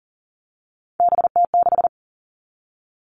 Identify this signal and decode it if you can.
Morse code, heard as a keyed tone, about 40 wpm, 700 Hz. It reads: BT6
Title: All Cut Numbers - 3-Digit 40wpm